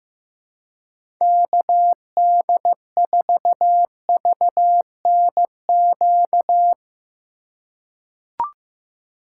Morse code, heard as a keyed tone, about 15 wpm, 700 Hz. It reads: KD4VNQ
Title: USA Call Signs - 6 Characters - 3x Character Spacing - 15wpm